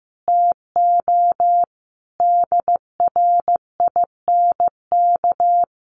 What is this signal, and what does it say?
Morse code, heard as a keyed tone, about 14 wpm, 700 Hz. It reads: TO DRINK
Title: Sentences from Top 1500 Words 15wpm